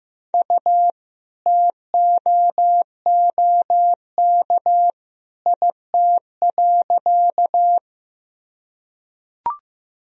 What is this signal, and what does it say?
Morse code, heard as a keyed tone, about 15 wpm, 700 Hz. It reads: U TOOK IT.
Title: Sentences from Top 300 Words and Abbreviations 15wpm